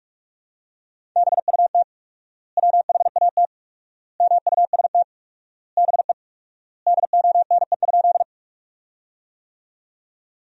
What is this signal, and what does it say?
Morse code, heard as a keyed tone, about 41 wpm, 700 Hz. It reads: BUT WHAT MUST BE DONE?
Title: 4x Word-Spacing - Sentences from Top 500 Words - 45wpm